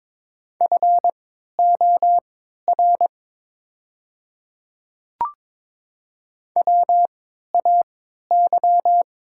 Morse code, heard as a keyed tone, about 22 wpm, spaced as for 12 wpm, 700 Hz. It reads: FOR WAY
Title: 3x Character Spacing - Top 100 Words - 22wpm